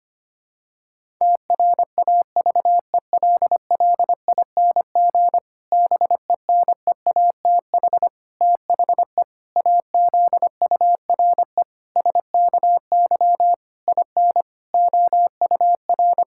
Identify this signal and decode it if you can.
Morse code, heard as a keyed tone, about 25 wpm, 700 Hz. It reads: TRAVELLING BENEATH THE AZURE SKY IN OUR
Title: Pangram Challenge v2 - 25wpm